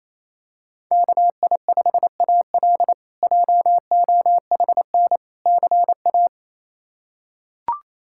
Morse code, heard as a keyed tone, about 28 wpm, 700 Hz. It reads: KI5AL JOHN CA
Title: K1USN SST QSO Practice - 28wpm